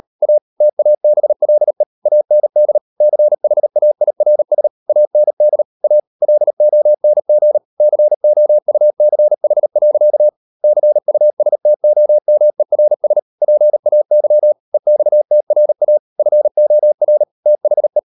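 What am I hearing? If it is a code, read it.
Morse code, heard as a keyed tone, about 38 wpm, 600 Hz. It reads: A TABLE AND CHAIRS AND A LONG COUCH. CUSTOMERS PAY EXTRA FOR THE
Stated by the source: Morse code practice at 38 WPM, sending Ken Follett's novel Never in English, from CW med Gnister.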